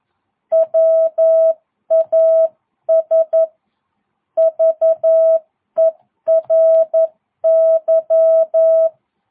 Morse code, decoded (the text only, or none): WAS VERY